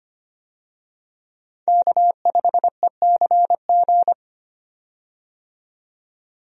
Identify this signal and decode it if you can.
Morse code, heard as a keyed tone, about 25 wpm, 700 Hz. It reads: K5ECG
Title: USA Call Signs - 5 Characters - 25wpm